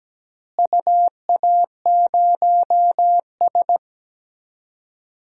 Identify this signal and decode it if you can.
Morse code, heard as a keyed tone, about 17 wpm, 700 Hz. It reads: UA0S
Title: DXCC-100 Call Signs - 4 Characters - Repeated 3x - 17wpm